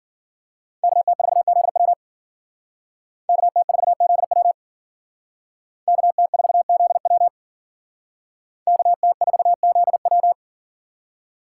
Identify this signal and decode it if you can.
Morse code, heard as a keyed tone, about 51 wpm, 700 Hz. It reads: KT4ZW KT4ZW KT4ZW KT4ZW